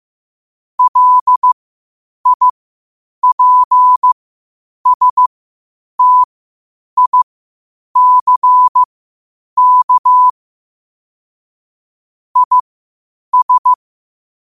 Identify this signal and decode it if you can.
Morse code, heard as a keyed tone, about 15 wpm, spaced as for 8 wpm, 1000 Hz. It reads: LIPSTICK IS
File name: qotdfiles / 8wpm